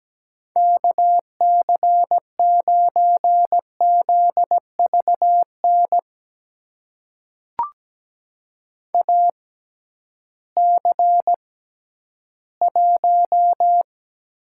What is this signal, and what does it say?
Morse code, heard as a keyed tone, about 17 wpm, spaced as for 11 wpm, 700 Hz. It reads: KC9ZVN AC1
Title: USA Call Signs - 6x Character Spacing - 17wpm